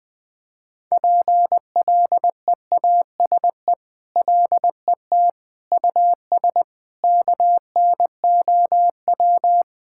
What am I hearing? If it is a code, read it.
Morse code, heard as a keyed tone, about 20 wpm, 700 Hz. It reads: PLEASE LET US KNOW
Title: Sentences with 8-Word Phrases - 20wpm